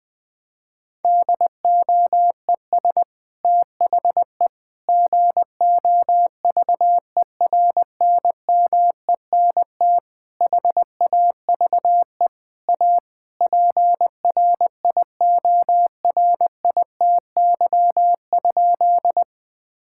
Morse code, heard as a keyed tone, about 20 wpm, 700 Hz. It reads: DOES THE GOVERNMENT HAVE A PRIORITY?
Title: Sentences from Top 1600 Words 20wpm